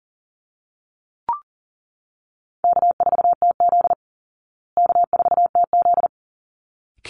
Morse code, heard as a keyed tone, about 40 wpm, 700 Hz. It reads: K4TZ K4TZ
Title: Call Signs Repeated 3x - 40wpm